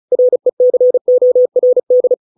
Morse code, heard as a keyed tone, about 34 wpm, 500 Hz. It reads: RECORD